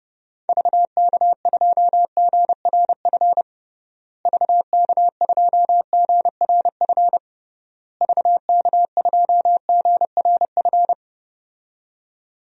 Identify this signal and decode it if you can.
Morse code, heard as a keyed tone, about 30 wpm, 700 Hz. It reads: VK2GRF VK2GRF VK2GRF
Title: DXCC-100 Call Signs - 6 Characters - Repeated 3x - 30wpm